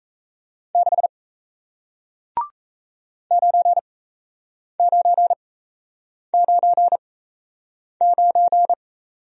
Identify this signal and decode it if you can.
Morse code, heard as a keyed tone, about 37 wpm, spaced as for 8 wpm, 700 Hz. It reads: 6 9999